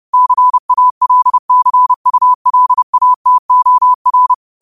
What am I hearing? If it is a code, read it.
Morse code, heard as a keyed tone, about 30 wpm, 1000 Hz. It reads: CALCULATOR
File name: qotdfiles / 30wpm